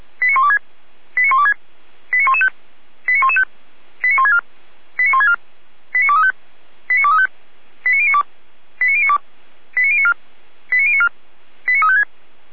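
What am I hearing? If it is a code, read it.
Five-tone selective call (ZVEI), heard as quick runs of five tones, tones of about 80 ms. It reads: 89126 89126 89115 89115 89164 89164 89235 89235 89092 89092 89094 89094 89356
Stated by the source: ZVEI five-tone call 89350 received on SDR.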